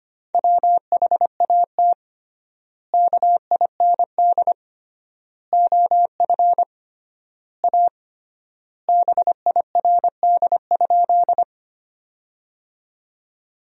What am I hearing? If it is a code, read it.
Morse code, heard as a keyed tone, about 25 wpm, 700 Hz. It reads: WHAT KIND OF A BIRD?
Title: 3x Word-Spacing - Sentences from Top 400 Words - 25wpm